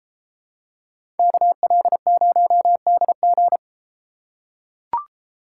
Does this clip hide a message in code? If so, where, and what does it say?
Yes: Morse code from 1.2 to 3.6 s, heard as a keyed tone, about 33 wpm, 700 Hz. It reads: KL0DG